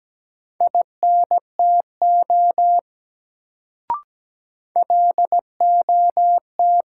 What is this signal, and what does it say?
Morse code, heard as a keyed tone, about 17 wpm, 700 Hz. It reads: INTO LOT
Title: Sets of 2 Words - New Word Drill - 17wpm